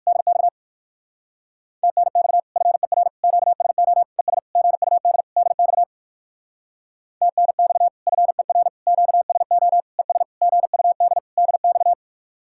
Morse code, heard as a keyed tone, about 64 wpm, spaced as for 56 wpm, 700 Hz. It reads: DX TNX FER QSO ES GUD DX TNX FER QSO ES GUD DX
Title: Speed-Racing Common QSO Phrases Encore - 45wpm